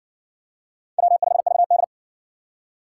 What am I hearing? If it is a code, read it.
Morse code, heard as a keyed tone, about 60 wpm, 700 Hz. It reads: W5VD